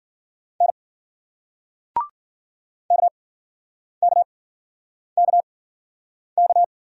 Morse code, heard as a keyed tone, about 50 wpm, spaced as for 9 wpm, 700 Hz. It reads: N KKKK